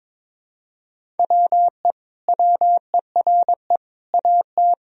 Morse code, heard as a keyed tone, about 22 wpm, 700 Hz. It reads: WE WERE AT
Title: Sentences from Top 1000 Words - Review Part 1 - 22wpm